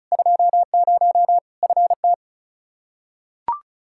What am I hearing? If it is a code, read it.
Morse code, heard as a keyed tone, about 36 wpm, 700 Hz. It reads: 20 FT
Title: Practice QSO - Exchange 2 - Info - 35wpm